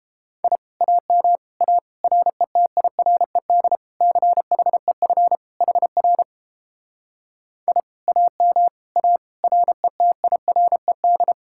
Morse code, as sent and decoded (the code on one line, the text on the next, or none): .. / .- -- / .- / .-. - .. .-. -.. / -.-. .... ..-. / .... .-. / .. / .- -- / .- / .-. - .. .-. -..
I AM A RETIRED CHEF HR I AM A RETIRED